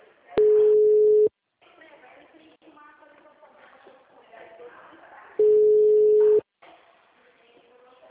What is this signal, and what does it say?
Ringback tone (Europe, 425 Hz), with speech.